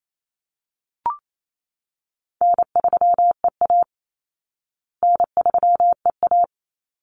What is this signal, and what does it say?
Morse code, heard as a keyed tone, about 28 wpm, 700 Hz. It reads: N3EA N3EA
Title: Easy Call Signs Repeated 3x - 28wpm